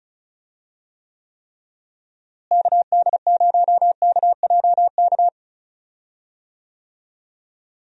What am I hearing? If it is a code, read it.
Morse code, heard as a keyed tone, about 36 wpm, 700 Hz. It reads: KD0KJK
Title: USA Call Signs - 6 Characters - Repeated 3x - 35wpm